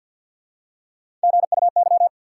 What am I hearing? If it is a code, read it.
Morse code, heard as a keyed tone, about 53 wpm, 700 Hz. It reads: GUY
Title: Three Letter Words - 50wpm